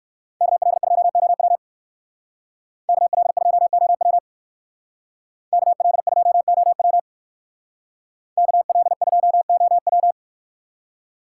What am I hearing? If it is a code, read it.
Morse code, heard as a keyed tone, about 59 wpm, 700 Hz. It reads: KL2OW KL2OW KL2OW KL2OW